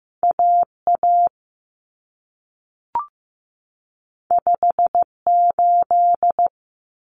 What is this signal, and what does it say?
Morse code, heard as a keyed tone, about 15 wpm, 700 Hz. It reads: AA 58
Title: Common Cut Numbers - 2-Digit 15wpm